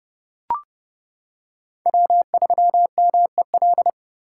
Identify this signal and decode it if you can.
Morse code, heard as a keyed tone, about 30 wpm, 700 Hz. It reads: W3MEL
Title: K1USN SST QSO Practice - 30wpm